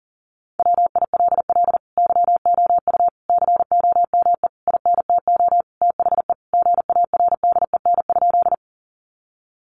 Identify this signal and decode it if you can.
Morse code, heard as a keyed tone, about 40 wpm, 700 Hz. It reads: WILL YOU COME INTO THE GARDEN?
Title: Sentences from Top 700 Words 40wpm